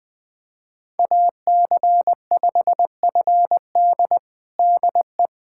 Morse code, heard as a keyed tone, about 21 wpm, 700 Hz. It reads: AC5FD DE
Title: USA Callsign DE Callsign - 5 Characters - 20wpm